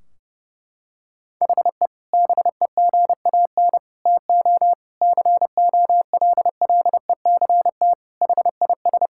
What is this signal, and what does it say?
Morse code, heard as a keyed tone, about 30 wpm, 700 Hz. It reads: HE BEGAN TO COLLECT HIS
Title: Sentences from Top 1600 Words 30wpm